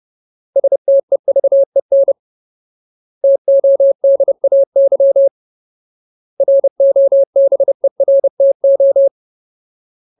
Morse code, heard as a keyed tone, about 30 wpm, 550 Hz. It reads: STEVEN TODAY ROBERTO
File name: Ultimate Word List / 30WPM Ultimate